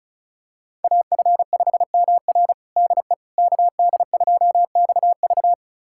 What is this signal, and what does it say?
Morse code, heard as a keyed tone, about 36 wpm, 690 Hz. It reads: AF5MR DE KD2XV
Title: USA Callsign DE Callsign - 5 Characters - 35wpm